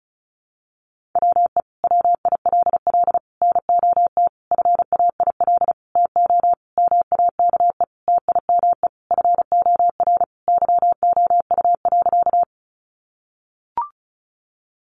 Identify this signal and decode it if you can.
Morse code, heard as a keyed tone, about 34 wpm, 700 Hz. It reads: WE WILL NOT FAIL TO MAKE TIME FOR YOU.
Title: Sentences from Top 1100 Words 35wpm